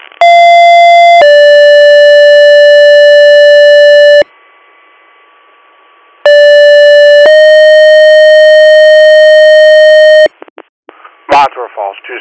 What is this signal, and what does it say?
Two-tone page, heard as a one-second tone, then a three-second tone, about 690 then 580 Hz, about 580 then 620 Hz, with speech. It is heard from 0.2 to 10.3 s.